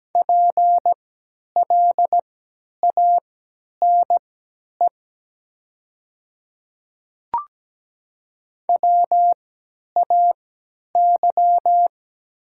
Morse code, heard as a keyed tone, about 17 wpm, spaced as for 10 wpm, 700 Hz. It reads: PLANE WAY
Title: 3x Character Spacing - Top 500 Words - 17wpm